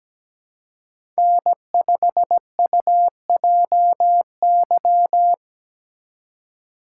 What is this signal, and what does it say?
Morse code, heard as a keyed tone, about 19 wpm, 700 Hz. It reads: N5UJY
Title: Speed Racing - USA 1x3 Call Signs 17wpm